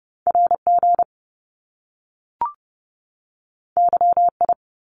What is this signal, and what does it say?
Morse code, heard as a keyed tone, about 30 wpm, 700 Hz. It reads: RG YI
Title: Top Two Letter Combinations in Words 30wpm